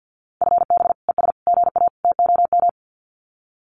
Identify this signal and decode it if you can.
Morse code, heard as a keyed tone, about 48 wpm, 700 Hz. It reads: FB ES GA TOM
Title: Common QSO Phrases Encore 50wpm